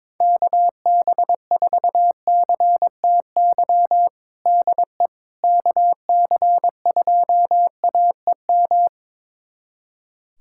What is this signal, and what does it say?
Morse code, heard as a keyed tone, about 22 wpm, 700 Hz. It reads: KB4CTY DE KC2AEM
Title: USA Callsign DE Callsign 22wpm